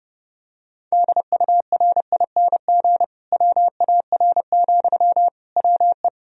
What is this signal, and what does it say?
Morse code, heard as a keyed tone, about 30 wpm, 700 Hz. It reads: DURING WAR, WE
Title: Sentences from Top 1000 Words - Review Part 4 - 30wpm